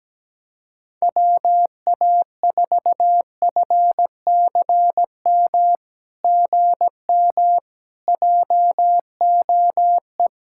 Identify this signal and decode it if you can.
Morse code, heard as a keyed tone, about 17 wpm, 700 Hz. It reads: WA4FCM GM JOE
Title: SOTA QSO Practice 17wpm